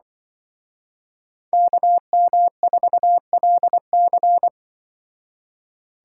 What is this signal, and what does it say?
Morse code, heard as a keyed tone, about 24 wpm, 700 Hz. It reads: KM4LC
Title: Speed Racing - USA 2x2 Call Signs 20wpm